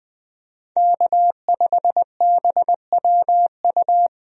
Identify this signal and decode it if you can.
Morse code, heard as a keyed tone, about 20 wpm, 700 Hz. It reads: K5BWU
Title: USA Call Signs - 5 Characters - Repeated 2x - 20wpm